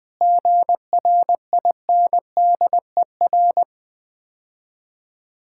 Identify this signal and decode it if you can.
Morse code, heard as a keyed tone, about 20 wpm, 700 Hz. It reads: GRINDER
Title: Easy Words - Speed-Racing with Hints - 20wpm